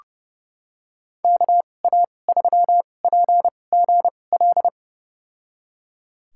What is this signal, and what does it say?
Morse code, heard as a keyed tone, about 30 wpm, spaced as for 22 wpm, 700 Hz. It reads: KA3PGL
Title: USA Call Signs - 6 Characters - 2x Character Spacing - 30wpm